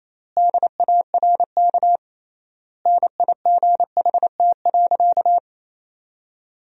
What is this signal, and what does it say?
Morse code, heard as a keyed tone, about 28 wpm, 700 Hz. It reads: DARK NIGHT.
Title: 3x Word-Spacing - Sentences from Top 500 Words - 28wpm